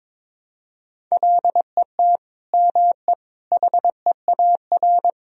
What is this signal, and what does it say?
Morse code, heard as a keyed tone, about 22 wpm, 700 Hz. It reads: LET ME HEAR
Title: Sentences from Top 1000 Words - Review Part 2 - 22wpm